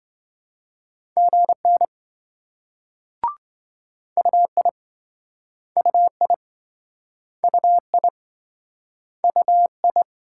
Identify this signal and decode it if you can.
Morse code, heard as a keyed tone, about 27 wpm, 700 Hz. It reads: GN UI UI UI UI